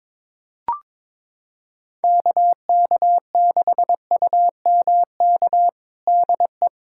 Morse code, heard as a keyed tone, about 22 wpm, 700 Hz. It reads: KK6UMK DE
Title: USA Callsign DE Callsign 22wpm